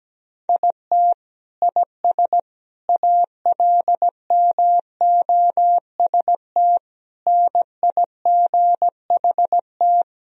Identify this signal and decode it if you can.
Morse code, heard as a keyed tone, about 17 wpm, 700 Hz. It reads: IT IS ALMOST NIGHT
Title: Sentences from Top 1000 Words - Review Part 4 - 17wpm